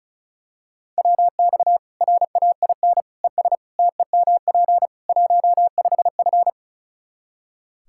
Morse code, heard as a keyed tone, about 34 wpm, 700 Hz. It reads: WX RAIN ES TEMP 15F
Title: Practice QSO - Exchange 2 - Info - 35wpm